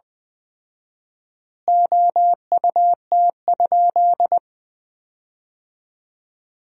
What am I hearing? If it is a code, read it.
Morse code, heard as a keyed tone, about 20 wpm, 700 Hz. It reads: OUT?
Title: 4x Word-Spacing - Sentences from Top 300 Words - 20wpm